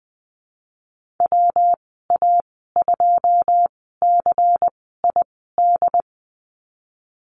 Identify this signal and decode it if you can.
Morse code, heard as a keyed tone, about 20 wpm, spaced as for 15 wpm, 700 Hz. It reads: WA2CID
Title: 147 Lesson 24g - Letter y - Call Signs 20wpm